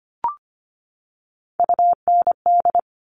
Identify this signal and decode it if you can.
Morse code, heard as a keyed tone, about 25 wpm, 700 Hz. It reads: UND
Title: Top Three Letter Combinations in Words 25wpm